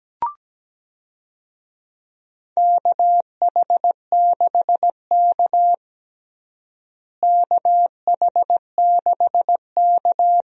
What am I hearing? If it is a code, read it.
Morse code, heard as a keyed tone, about 17 wpm, 700 Hz. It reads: KH6K KH6K
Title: DXCC-100 Call Signs - 4 Characters - Repeated 2x - 17wpm